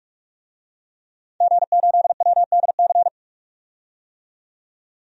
Morse code, heard as a keyed tone, about 46 wpm, 700 Hz. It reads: G8WDC
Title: DXCC-100 Call Signs - 5 Characters - 45wpm